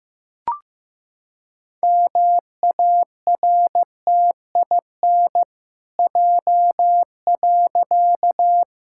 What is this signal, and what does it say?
Morse code, heard as a keyed tone, about 15 wpm, 700 Hz. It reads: MARTIN J.
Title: Pangram Challenge v2 - 15wpm